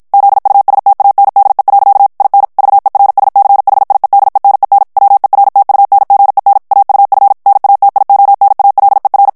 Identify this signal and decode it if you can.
Morse code, heard as a keyed tone, about 54 wpm, 800 Hz. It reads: ZUSTANDE. IN VERSCHIEDENEN WERTUNGEN AUF NATIONALER
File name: morse mp3 / ALLEMAND texte clair 2 vitesse 50 mots minute